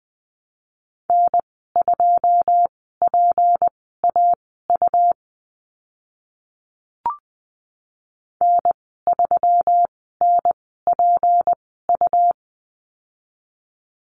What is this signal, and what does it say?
Morse code, heard as a keyed tone, about 20 wpm, spaced as for 15 wpm, 700 Hz. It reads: N2PAU N3NPU
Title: 154 Lesson 25g - Letter p - Call Signs 20wpm